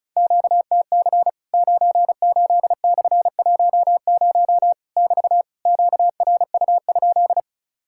Morse code, heard as a keyed tone, about 34 wpm, 700 Hz. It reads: QTC 98/10 <BT> QRU?
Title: Worked All Europe DX Contest - QTC Exchange - 35wpm